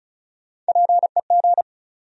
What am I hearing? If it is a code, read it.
Morse code, heard as a keyed tone, about 36 wpm, 700 Hz. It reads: PEG
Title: USA Common Names - Max 4 Letters - 35wpm